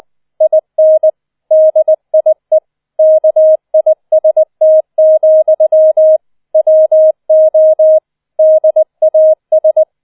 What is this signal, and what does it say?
Morse code, heard as a keyed tone, about 20 wpm, 630 Hz. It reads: IN DIE KIST, WO DAS